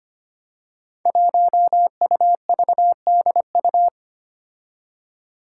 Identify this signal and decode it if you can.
Morse code, heard as a keyed tone, about 25 wpm, 700 Hz. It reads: 1UVDU
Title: Five Challenging Letter-Number Combinations 25wpm